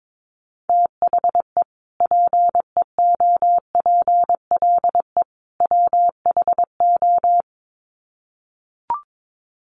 Morse code, heard as a keyed tone, about 22 wpm, 700 Hz. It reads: THE PEOPLE WHO
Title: Sets of 3 Words 22wpm